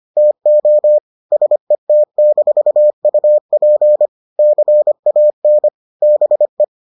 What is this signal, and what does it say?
Morse code, heard as a keyed tone, about 25 wpm, 600 Hz. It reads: TO SET-UP CAN BE